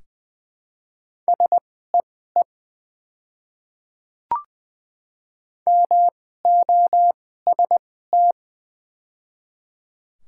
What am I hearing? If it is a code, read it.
Morse code, heard as a keyed tone, about 20 wpm, spaced as for 14 wpm, 700 Hz. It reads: SEE MOST
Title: 2x Character Spacing - Top 200 Words - 20wpm